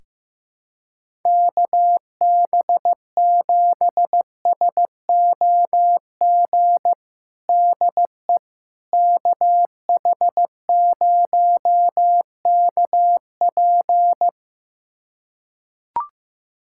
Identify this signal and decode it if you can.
Morse code, heard as a keyed tone, about 15 wpm, 700 Hz. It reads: KB7SOG DE KH0KP
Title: USA Callsign DE Callsign 15wpm